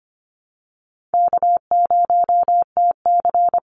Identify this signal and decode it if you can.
Morse code, heard as a keyed tone, about 25 wpm, 700 Hz. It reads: K0TC
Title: Easy Call Signs Repeated 2x - 25wpm